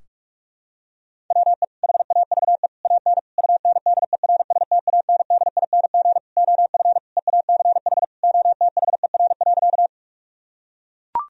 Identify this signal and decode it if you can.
Morse code, heard as a keyed tone, about 42 wpm, 700 Hz. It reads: WE HAVE AN UNDERSTANDING OF EACH OTHER.
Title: Sentences from Top 1000 Words - Review Part 3 - 45wpm